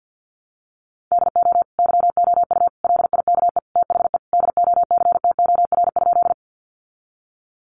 Morse code, heard as a keyed tone, about 49 wpm, 700 Hz. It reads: DO YOU LIKE THE DOCTOR?
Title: Sentences from Top 1500 Words 50wpm